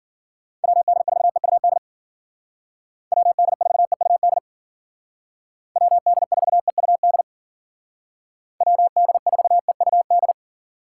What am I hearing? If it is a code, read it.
Morse code, heard as a keyed tone, about 48 wpm, 700 Hz. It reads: WD4EUD WD4EUD WD4EUD WD4EUD
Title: Speed Racing - USA 2x3 Call Signs 40wpm